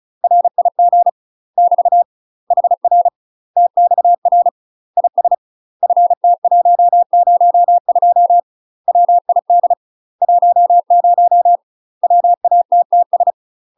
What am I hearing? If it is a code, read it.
Morse code, heard as a keyed tone, about 36 wpm, 700 Hz. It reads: RIG <BT> HR TXR IS FT102 WID 10 WATTS
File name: UK MASTER QSOs TRAINING / 35WPM / UK MASTER QSOs 0011